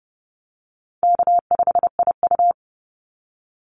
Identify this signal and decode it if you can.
Morse code, heard as a keyed tone, about 30 wpm, 700 Hz. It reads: K5IU
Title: Call Signs - 30wpm